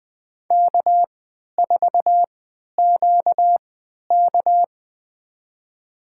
Morse code, heard as a keyed tone, about 21 wpm, spaced as for 11 wpm, 700 Hz. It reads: K4QK
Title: USA Call Signs - 4 Characters - 3x Character Spacing - 20wpm